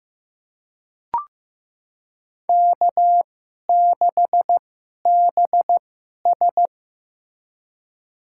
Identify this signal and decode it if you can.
Morse code, heard as a keyed tone, about 15 wpm, spaced as for 11 wpm, 700 Hz. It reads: K6BS
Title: USA Call Signs - 4 Characters - 2x Character Spacing - 15wpm